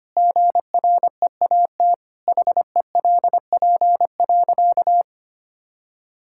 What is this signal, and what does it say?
Morse code, heard as a keyed tone, about 25 wpm, 700 Hz. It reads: GREAT HELP.
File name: practice / 25wpm / Sentences with 3-Word Phrases - 25wpm